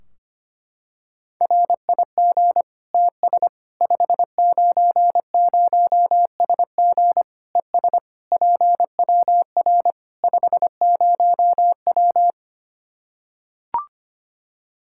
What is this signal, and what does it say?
Morse code, heard as a keyed tone, about 25 wpm, 700 Hz. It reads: RIG TS 590SG ES PWR 50W
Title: K1USN SST QSO Practice 25wpm